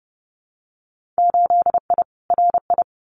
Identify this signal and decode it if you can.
Morse code, heard as a keyed tone, about 30 wpm, 700 Hz. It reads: 8I RI
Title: Winter Field Day Exchanges Encore 30wpm